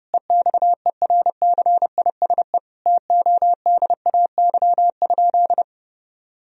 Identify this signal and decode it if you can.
Morse code, heard as a keyed tone, about 30 wpm, 700 Hz. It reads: EXERCISE TODAY?
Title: Sentences from Top 1000 Words - Review Part 4 - 30wpm